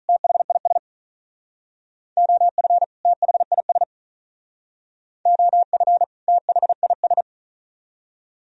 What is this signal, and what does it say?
Morse code, heard as a keyed tone, about 41 wpm, 700 Hz. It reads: THIS OF THIS OF THIS
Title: Speed-Racing - Sets of 2 Words 35wpm